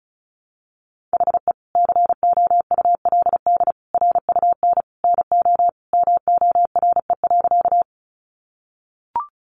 Morse code, heard as a keyed tone, about 34 wpm, 700 Hz. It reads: HE COULD RUN NO MORE.
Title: Sentences from Top 300 Words 35wpm